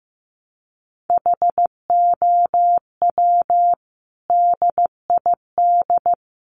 Morse code, heard as a keyed tone, about 15 wpm, 700 Hz. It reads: HOW DID
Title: Sentences from Top 700 Words 15wpm